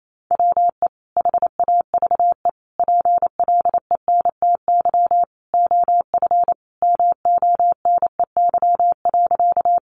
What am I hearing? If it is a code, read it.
Morse code, heard as a keyed tone, about 28 wpm, 700 Hz. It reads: WE HAVE PLENTY OF MONEY.
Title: Sentences from Top 1300 Words 28wpm